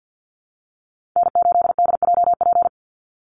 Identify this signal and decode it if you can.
Morse code, heard as a keyed tone, about 50 wpm, 700 Hz. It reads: N8DJP
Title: USA Call Signs 50wpm